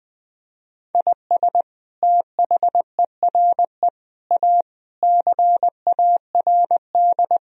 Morse code, heard as a keyed tone, about 20 wpm, 700 Hz. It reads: IS THERE A CARD
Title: Sentences from Top 1000 Words - Review Part 4 - 20wpm